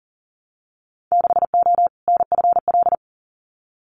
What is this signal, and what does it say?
Morse code, heard as a keyed tone, about 40 wpm, 700 Hz. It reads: 6O NFL
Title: Winter Field Day Exchanges 40wpm